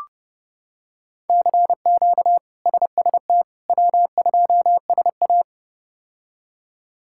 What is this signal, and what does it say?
Morse code, heard as a keyed tone, about 32 wpm, 700 Hz. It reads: CQ SST W2SA
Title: K1USN SST QSO Practice - 30wpm